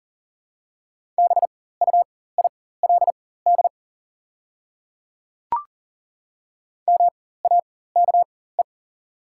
Morse code, heard as a keyed tone, about 40 wpm, spaced as for 19 wpm, 690 Hz. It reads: BUILD MAKE
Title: 4x Character Spacing - Top 200 Words - 40wpm